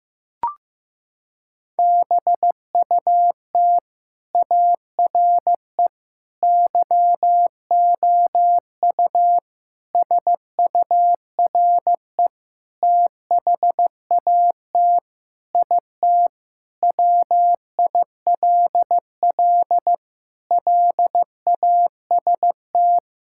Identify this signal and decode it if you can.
Morse code, heard as a keyed tone, about 15 wpm, 700 Hz. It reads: BUT ARE YOU SURE THAT IT WILL LAST
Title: Sentences from Top 1600 Words 15wpm